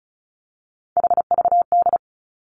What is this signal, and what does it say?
Morse code, heard as a keyed tone, about 36 wpm, 700 Hz. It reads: HVD
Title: Three Challenging Letter-Number Combinations 35wpm